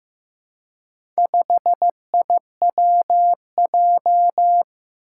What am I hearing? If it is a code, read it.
Morse code, heard as a keyed tone, about 16 wpm, 700 Hz. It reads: 5IWJ
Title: Four Challenging Letter-Number Combinations 15wpm